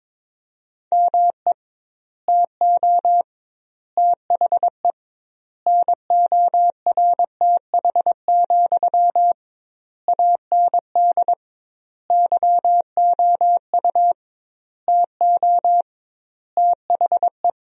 Morse code, heard as a keyed tone, about 22 wpm, 700 Hz. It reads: ME TO THE NORTH, AND YOU TO THE
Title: 2x Word-Spacing - Sentences from Top 400 Words - 22wpm